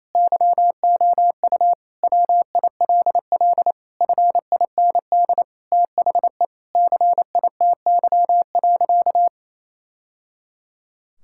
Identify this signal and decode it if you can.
Morse code, heard as a keyed tone, about 28 wpm, 700 Hz. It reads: YOU WILL FIND THE CITY.
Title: Sentences from Top 1000 Words - Review Part 6 - 28wpm